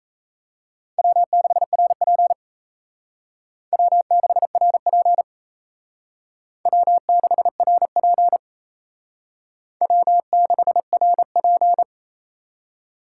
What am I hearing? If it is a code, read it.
Morse code, heard as a keyed tone, about 34 wpm, 700 Hz. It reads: W6RP W6RP W6RP W6RP
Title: Speed Racing - USA 1x2 Call Signs 28wpm